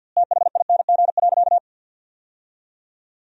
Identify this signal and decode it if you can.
Morse code, heard as a keyed tone, about 50 wpm, 700 Hz. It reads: THING.
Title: 2x Word-Spacing - Sentences from Top 300 Words - 50wpm